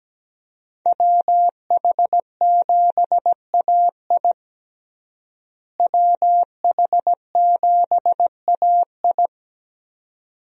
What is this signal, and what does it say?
Morse code, heard as a keyed tone, about 17 wpm, 700 Hz. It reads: WH7AI WH7AI
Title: USA Call Signs - 5 Characters - Repeated 2x - 17wpm